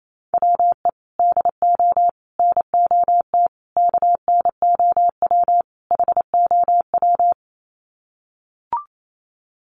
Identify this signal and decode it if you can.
Morse code, heard as a keyed tone, about 28 wpm, 700 Hz. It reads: WE DO NOT KNOW HOW
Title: Sets of 5 Words Encore 28wpm